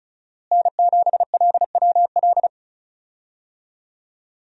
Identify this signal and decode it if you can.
Morse code, heard as a keyed tone, about 36 wpm, 700 Hz. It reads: N7LWL
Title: USA Call Signs - 5 Characters - Repeated 3x - 35wpm